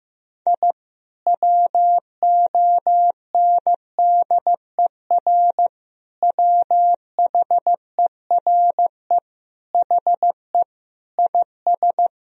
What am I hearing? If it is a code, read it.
Morse code, heard as a keyed tone, about 16 wpm, 700 Hz. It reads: I WONDER WHERE HE IS
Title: Sentences from Top 1000 Words - Review Part 3 - 15wpm